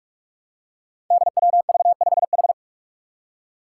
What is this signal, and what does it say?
Morse code, heard as a keyed tone, about 46 wpm, 700 Hz. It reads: DWV5H
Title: Five Challenging Letter-Number Combinations 45wpm